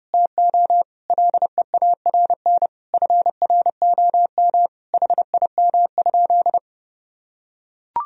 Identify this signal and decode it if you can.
Morse code, heard as a keyed tone, about 32 wpm, 700 Hz. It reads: TO LEARN FROM HIM?
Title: Sentences from Top 1000 Words - Review Part 2 - 30wpm